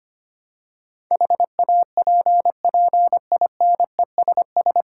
Morse code, heard as a keyed tone, about 25 wpm, 700 Hz. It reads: HAPPINESS
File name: practice / 25wpm / 9 Letter Words - First 3 Letters Given - 25wpm